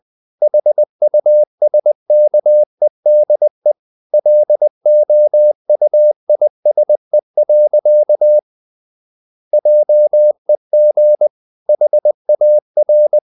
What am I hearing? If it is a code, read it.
Morse code, heard as a keyed tone, about 20 wpm, 600 Hz. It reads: HUSKEDE LOUISE. JEG HAR
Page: Ferie 20 wpm